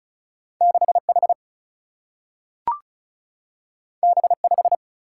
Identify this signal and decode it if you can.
Morse code, heard as a keyed tone, about 36 wpm, 700 Hz. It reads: 6H B5
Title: Single-Letter Single-Number 35wpm